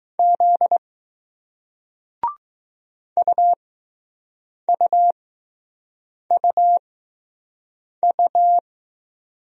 Morse code, heard as a keyed tone, about 20 wpm, spaced as for 6 wpm, 700 Hz. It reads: Z UUUU